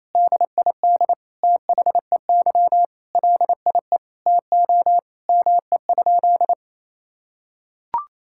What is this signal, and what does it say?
Morse code, heard as a keyed tone, about 28 wpm, 700 Hz. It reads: DID THEY LIE TO ME?
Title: Sentences from Top 1600 Words 28wpm